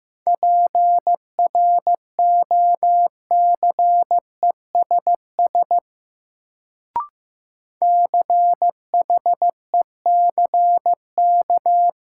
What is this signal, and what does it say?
Morse code, heard as a keyed tone, about 15 wpm, 700 Hz. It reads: PROCESS CHECK
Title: Sentences from Top 1000 Words - Review Part 3 - New Word Drill - 15wpm